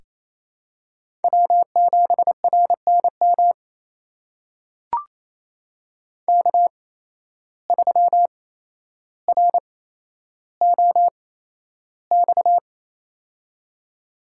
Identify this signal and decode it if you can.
Morse code, heard as a keyed tone, about 28 wpm, spaced as for 12 wpm, 700 Hz. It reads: W7RNM K3ROX
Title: USA Call Signs - 5 Characters - 8x Character Spacing - 28wpm